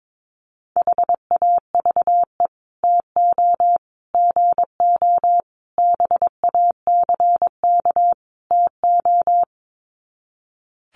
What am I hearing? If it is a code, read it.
Morse code, heard as a keyed tone, about 22 wpm, 700 Hz. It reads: HAVE TO GO BACK TO
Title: Sets of 5 Words Encore2 22wpm